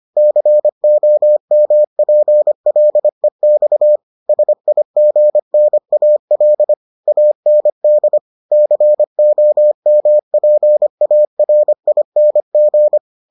Morse code, heard as a keyed tone, about 25 wpm, 600 Hz. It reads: COMPLEX SIGNAL AND COMPARING